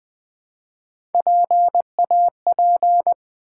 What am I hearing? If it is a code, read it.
Morse code, heard as a keyed tone, about 20 wpm, 700 Hz. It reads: PAP
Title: Build A Word - 1 Letter At A Time - 5-Letter Words - 20wpm